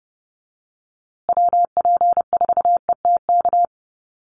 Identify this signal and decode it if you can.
Morse code, heard as a keyed tone, about 30 wpm, 700 Hz. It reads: WP4ETK
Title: USA Call Signs Repeated 2x 30wpm